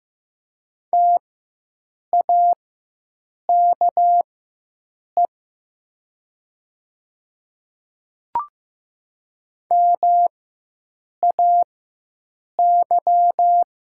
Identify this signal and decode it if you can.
Morse code, heard as a keyed tone, about 15 wpm, spaced as for 7 wpm, 690 Hz. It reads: TAKE MAY
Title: 4x Character Spacing - Top 200 Words - 15wpm